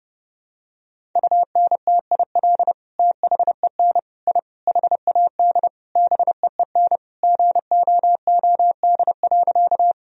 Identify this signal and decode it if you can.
Morse code, heard as a keyed tone, about 30 wpm, 700 Hz. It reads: UNTIL THEN I HAD BEEN GOOD.
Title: Sentences from Top 1000 Words - Review Part 1 - 30wpm